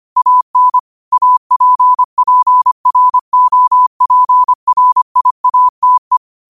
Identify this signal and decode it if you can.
Morse code, heard as a keyed tone, about 25 wpm, 1000 Hz. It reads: AN APPROPRIATE
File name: qotdfiles / 25wpm